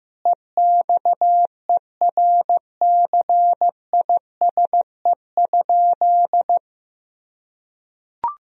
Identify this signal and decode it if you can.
Morse code, heard as a keyed tone, about 15 wpm, 700 Hz. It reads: EXERCISE?
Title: Sentences from Top 1000 Words - Review Part 4 - 15wpm